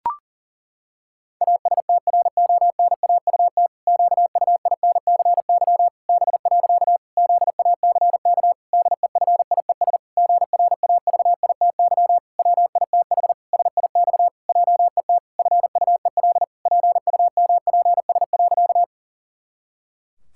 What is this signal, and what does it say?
Morse code, heard as a keyed tone, about 40 wpm, 700 Hz. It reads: ASTRONAUT QUINCY B. ZACK DEFIES GRAVITY WITH SIX JET FUEL PUMPS.
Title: Pangram Challenge v2 - 40wpm